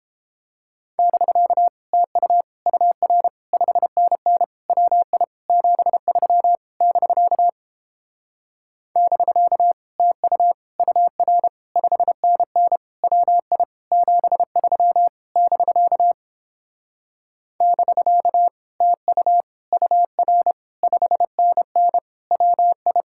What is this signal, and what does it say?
Morse code, heard as a keyed tone, about 29 wpm, 700 Hz. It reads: <BK> TU UR 5NN WI 73 <BK> <BK> TU UR 5NN WI 73 <BK> <BK> TU UR 5NN WI